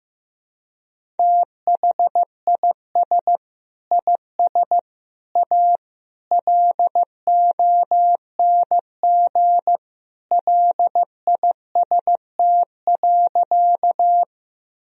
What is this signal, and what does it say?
Morse code, heard as a keyed tone, about 15 wpm, 700 Hz. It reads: THIS IS A LONG LIST.
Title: Sentences from Top 1000 Words - Review Part 2 - 15wpm